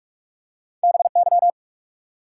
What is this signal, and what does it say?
Morse code, heard as a keyed tone, about 46 wpm, 700 Hz. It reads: BY
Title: Top 90 QSO Elements 45wpm